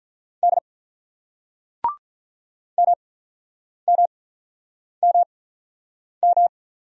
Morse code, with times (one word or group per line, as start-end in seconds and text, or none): 0.4-0.6 D
2.8-6.5 MMMM